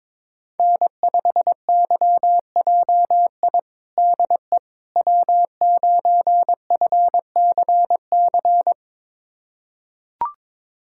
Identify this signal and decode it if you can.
Morse code, heard as a keyed tone, about 22 wpm, 700 Hz. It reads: N5YJI DE W9FCC
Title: USA Callsign DE Callsign - 5 Characters - 22wpm